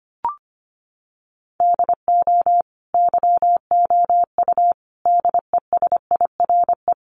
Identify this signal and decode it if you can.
Morse code, heard as a keyed tone, about 25 wpm, 700 Hz. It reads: DO YOU DESIRE
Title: Sentences from Top 1100 Words 25wpm